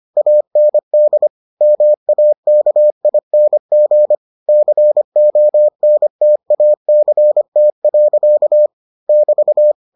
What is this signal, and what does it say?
Morse code, heard as a keyed tone, about 25 wpm, 600 Hz. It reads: AND MAKING CONTACT. <BT>